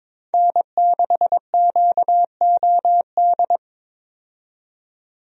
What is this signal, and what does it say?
Morse code, heard as a keyed tone, about 21 wpm, 700 Hz. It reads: N6QOD
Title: USA Call Signs - 5 Characters - 22wpm